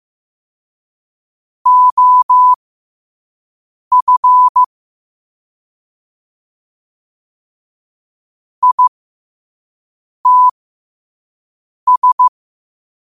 Morse code, heard as a keyed tone, about 15 wpm, spaced as for 5 wpm, 1000 Hz. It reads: OF ITS
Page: New quotes every day in morse code at 5 Words per minute.